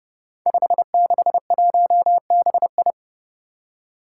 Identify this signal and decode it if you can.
Morse code, heard as a keyed tone, about 30 wpm, 700 Hz. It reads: 561BI
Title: Five Challenging Letter-Number Combinations 30wpm